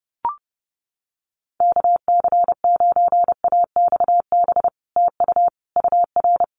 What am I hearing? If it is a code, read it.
Morse code, heard as a keyed tone, about 30 wpm, 700 Hz. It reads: KC9AXB TU UR
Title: Common QSO Phrases Encore 30wpm